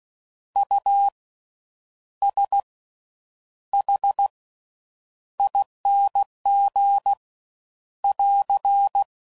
Morse code, heard as a keyed tone, about 16 wpm, spaced as for 8 wpm, 800 Hz. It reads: USHING <AR>
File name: Suffixes / Special INGs / Slow With Faster Ings